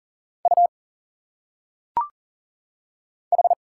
Morse code, heard as a keyed tone, about 40 wpm, 700 Hz. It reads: U H
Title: Single Letters - Phonetic Pronunciation 40wpm